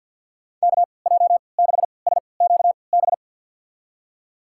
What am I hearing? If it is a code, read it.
Morse code, heard as a keyed tone, about 48 wpm, spaced as for 28 wpm, 700 Hz. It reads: KJ6SQB
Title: USA Call Signs - 6 Characters - 3x Character Spacing - 50wpm